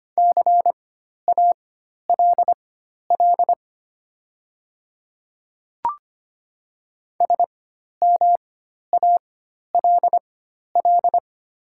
Morse code, heard as a keyed tone, about 25 wpm, spaced as for 12 wpm, 700 Hz. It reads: CALL SMALL
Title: 4x Character Spacing - Top 200 Words - 25wpm